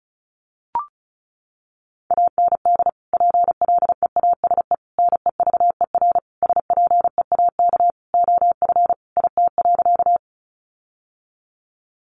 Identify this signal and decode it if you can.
Morse code, heard as a keyed tone, about 34 wpm, 700 Hz. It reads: AND PLEASE NEVER SPEAK OF IT.
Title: Sentences from Top 1500 Words 35wpm